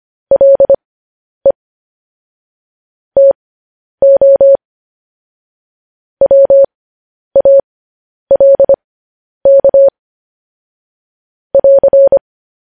Morse code, heard as a keyed tone, about 25 wpm, spaced as for 10 wpm, 550 Hz. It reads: LE TO WALK <AR>
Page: news headlines in morse code at 10 wpm